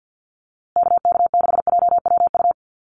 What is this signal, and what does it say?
Morse code, heard as a keyed tone, about 48 wpm, 700 Hz. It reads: KK6JWU
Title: USA Call Signs Repeated 3x 50wpm